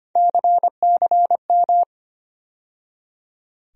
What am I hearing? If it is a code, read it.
Morse code, heard as a keyed tone, about 25 wpm, 700 Hz. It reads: CCM
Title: USA Call Sign Postfixes - 3 Letters 25wpm